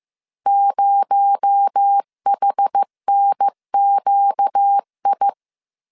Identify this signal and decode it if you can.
Morse code, heard as a keyed tone, about 15 wpm, 800 Hz. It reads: 0HNQI
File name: CW / DET / 1 9 1500C S B 14wpm 3302